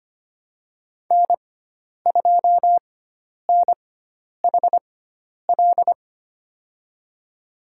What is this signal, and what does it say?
Morse code, heard as a keyed tone, about 25 wpm, spaced as for 10 wpm, 700 Hz. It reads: N2NHL
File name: Lesson 16g - Letter c - Call Signs 25wpm